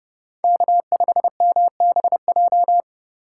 Morse code, heard as a keyed tone, about 30 wpm, 700 Hz. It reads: K5MBJ
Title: USA Call Signs - 5 Characters - 3x Character Spacing - 30wpm